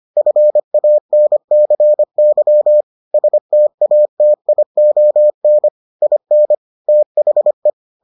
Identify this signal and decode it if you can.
Morse code, heard as a keyed tone, about 25 wpm, 600 Hz. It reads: FANCY STATION IN THE